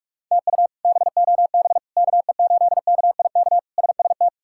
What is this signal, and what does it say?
Morse code, heard as a keyed tone, about 46 wpm, 700 Hz. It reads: TU BOB KE8KIK SST
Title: K1USN SST QSO Practice - 45wpm